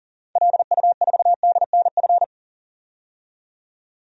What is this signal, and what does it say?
Morse code, heard as a keyed tone, about 40 wpm, 700 Hz. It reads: LU4DNF
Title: DXCC-100 Call Signs - 6 Characters - 40wpm